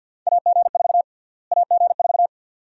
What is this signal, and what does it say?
Morse code, heard as a keyed tone, about 51 wpm, 700 Hz. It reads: AG4 AG4
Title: USA Call Sign Prefixes - Repeated 2x - 50wpm